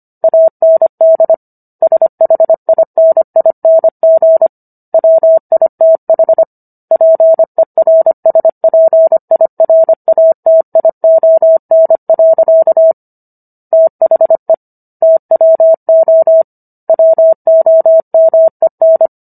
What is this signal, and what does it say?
Morse code, heard as a keyed tone, about 25 wpm, 650 Hz. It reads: AND SHINING WITH PERSPIRATION. THE TWO WOMEN
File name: Book One / 16-Chapter 16 - 25 WPM